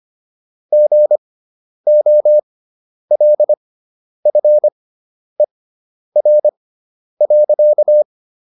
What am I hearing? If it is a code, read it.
Morse code, heard as a keyed tone, about 25 wpm, spaced as for 10 wpm, 600 Hz. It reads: GOLFER.